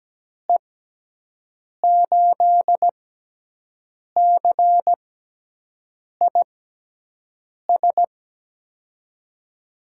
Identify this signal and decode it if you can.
Morse code, heard as a keyed tone, about 17 wpm, spaced as for 6 wpm, 700 Hz. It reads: E8CIS
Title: USA Call Signs - 6 Characters - 6x Character Spacing - 17wpm